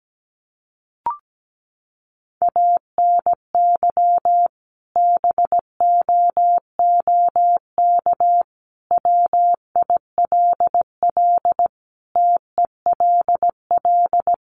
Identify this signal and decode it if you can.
Morse code, heard as a keyed tone, about 17 wpm, 700 Hz. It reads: ANY BOOK WILL TELL
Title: Sentences from Top 300 Words 17wpm